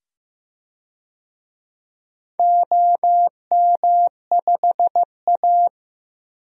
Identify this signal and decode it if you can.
Morse code, heard as a keyed tone, about 15 wpm, 700 Hz. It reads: OM5A
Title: DXCC-100 Call Signs - 4 Characters - 15wpm